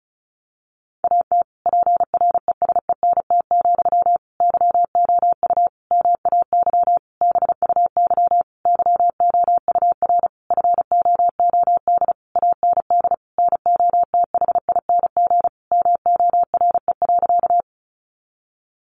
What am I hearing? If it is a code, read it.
Morse code, heard as a keyed tone, about 34 wpm, 700 Hz. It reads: AT PRESENT, YOU MAY BUY YOUR FOOD AND NOTHING MORE.
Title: Sentences from Top 800 Words 35wpm